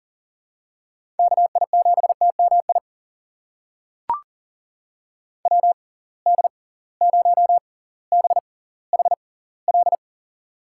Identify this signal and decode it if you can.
Morse code, heard as a keyed tone, about 40 wpm, spaced as for 20 wpm, 700 Hz. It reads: KI7TMI WD0BHL
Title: USA Call Signs - 6 Characters - 6x Character Spacing - 40wpm